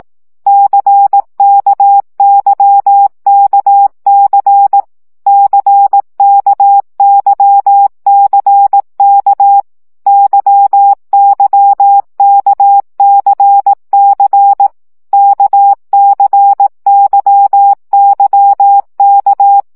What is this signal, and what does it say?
Morse code, heard as a keyed tone, about 18 wpm, 800 Hz. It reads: CKYKC CKYCK YYKCC KCYYK
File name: CW gyakorlas k c y 15 baud